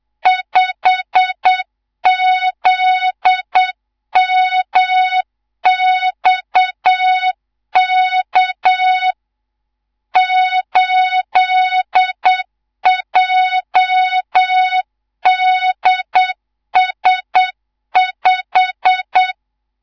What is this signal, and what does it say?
Morse code, heard as a keyed tone, about 8 wpm, 3700 Hz, with noise. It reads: 5ZMXK 8JDS5
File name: Morze-HA2MN / hang / 19 GYAKORLAT